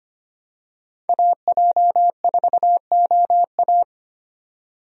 Morse code, heard as a keyed tone, about 25 wpm, 700 Hz. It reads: AJ4OA
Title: USA Call Signs - 5 Characters - 25wpm